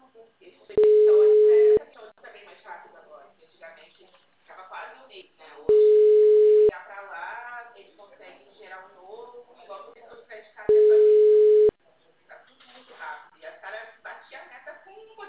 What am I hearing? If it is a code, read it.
Ringback tone (Europe, 425 Hz), with speech.